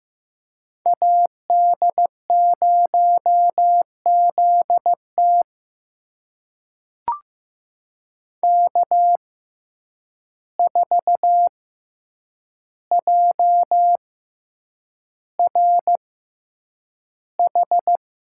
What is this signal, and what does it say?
Morse code, heard as a keyed tone, about 15 wpm, spaced as for 8 wpm, 700 Hz. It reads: AD0ZT K4JRH
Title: USA Call Signs - 5 Characters - 6x Character Spacing - 15wpm